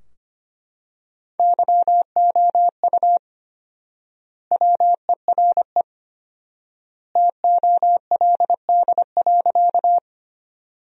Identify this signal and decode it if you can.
Morse code, heard as a keyed tone, about 25 wpm, 700 Hz. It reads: YOU WERE TOLD.
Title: 4x Word-Spacing - Sentences from Top 400 Words - 25wpm